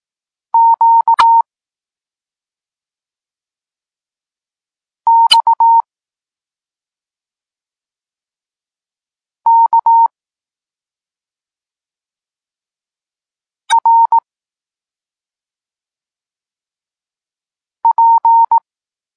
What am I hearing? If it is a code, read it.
Morse code, heard as a keyed tone, about 18 wpm, 920 Hz. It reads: Q X K R P
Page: Tekst 32 – Doskonalenie odbioru poznanych liter